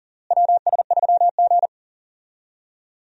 Morse code, heard as a keyed tone, about 40 wpm, 700 Hz. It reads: WS3G
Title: USA Call Signs - 4 Characters - Repeated 2x - 40wpm